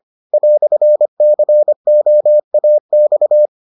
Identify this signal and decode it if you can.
Morse code, heard as a keyed tone, about 25 wpm, 600 Hz. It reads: "COAX